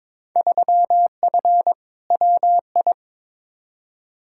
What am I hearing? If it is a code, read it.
Morse code, heard as a keyed tone, about 22 wpm, 700 Hz. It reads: 3F WI
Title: Field Day Exchanges - 22wpm